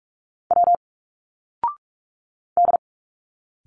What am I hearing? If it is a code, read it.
Morse code, heard as a keyed tone, about 46 wpm, 700 Hz. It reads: W D
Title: Single Letters 45wpm